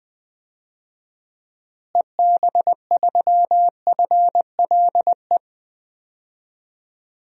Morse code, heard as a keyed tone, about 20 wpm, 700 Hz. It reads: EB3FLE
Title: DXCC-100 Call Signs - 6 Characters - Repeated 2x - 20wpm